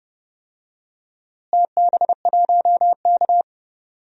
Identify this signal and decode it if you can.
Morse code, heard as a keyed tone, about 30 wpm, 700 Hz. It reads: TB1K
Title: DXCC-100 Call Signs - 4 Characters - Repeated 2x - 30wpm